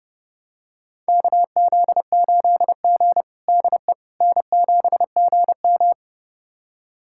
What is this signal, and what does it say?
Morse code, heard as a keyed tone, about 30 wpm, 700 Hz. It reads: KZ8G DE N7GM
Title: USA Callsign DE Callsign - 4 Characters - 30wpm